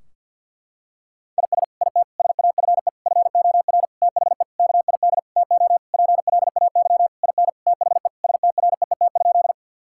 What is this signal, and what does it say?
Morse code, heard as a keyed tone, about 50 wpm, 700 Hz. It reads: IS IT SAFE FOR THE KID TO PLAY IN THE STREET?
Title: Sentences from Top 1600 Words 50wpm